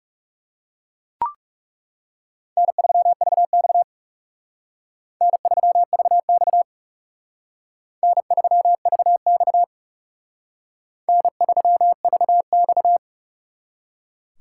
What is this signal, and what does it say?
Morse code, heard as a keyed tone, about 38 wpm, 700 Hz. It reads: N3VX N3VX N3VX N3VX
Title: Speed Racing - USA 1x2 Call Signs 30wpm